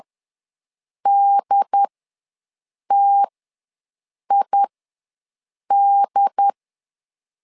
Morse code, heard as a keyed tone, about 10 wpm, spaced as for 6 wpm, 800 Hz. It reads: DTID
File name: LEZIONI / 2 UHDN / 2 UHD ws 10wpm 2058